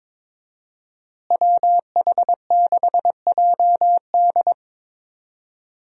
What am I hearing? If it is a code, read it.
Morse code, heard as a keyed tone, about 21 wpm, 700 Hz. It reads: WH6JD
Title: USA Call Signs - 5 Characters - Repeated 3x - 22wpm